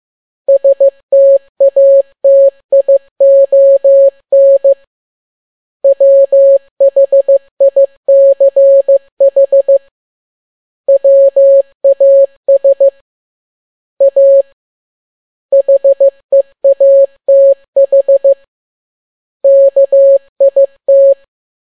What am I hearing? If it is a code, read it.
Morse code, heard as a keyed tone, about 15 wpm, 550 Hz. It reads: STATION WHICH WAS A HEATH KIT
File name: n6tt p1 15C13